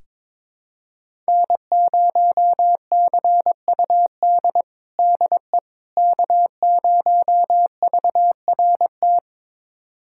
Morse code, heard as a keyed tone, about 22 wpm, 700 Hz. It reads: N0CUD DE K0VRT
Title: USA Callsign DE Callsign - 5 Characters - 22wpm